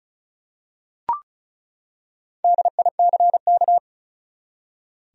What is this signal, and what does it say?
Morse code, heard as a keyed tone, about 34 wpm, 700 Hz. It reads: DICK
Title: USA Common Names - 35wpm